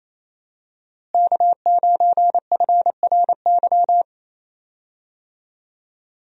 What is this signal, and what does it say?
Morse code, heard as a keyed tone, about 27 wpm, 700 Hz. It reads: K9FRY
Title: USA Call Signs - 5 Characters - Repeated 3x - 28wpm